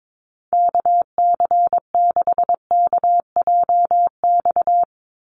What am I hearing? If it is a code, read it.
Morse code, heard as a keyed tone, about 22 wpm, 700 Hz. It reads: KC6KJX
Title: USA Call Signs 22wpm